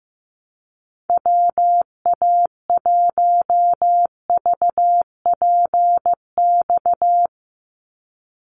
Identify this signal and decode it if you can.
Morse code, heard as a keyed tone, about 15 wpm, 700 Hz. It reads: WA1VPX
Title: USA Call Signs 15wpm